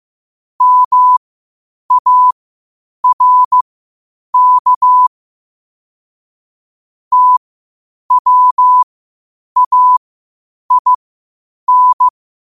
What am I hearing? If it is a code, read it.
Morse code, heard as a keyed tone, about 15 wpm, spaced as for 8 wpm, 1000 Hz. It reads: MARK TWAIN
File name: qotdfiles / 8wpm